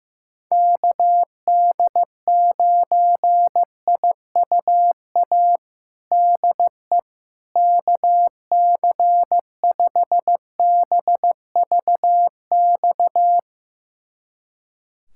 Morse code, heard as a keyed tone, about 15 wpm, 700 Hz. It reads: KD9IUA DE KC5BVX
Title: USA Callsign DE Callsign 15wpm